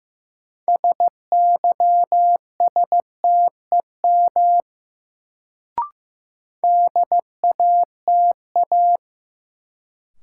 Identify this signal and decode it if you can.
Morse code, heard as a keyed tone, about 15 wpm, 700 Hz. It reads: SYSTEM DATA
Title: Sentences from Top 1000 Words - Review Part 3 - New Word Drill - 15wpm